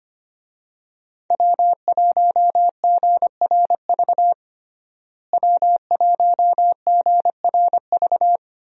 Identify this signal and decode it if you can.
Morse code, heard as a keyed tone, about 25 wpm, 700 Hz. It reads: W1GRV W1GRV
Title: USA Call Signs - 5 Characters - Repeated 2x - 25wpm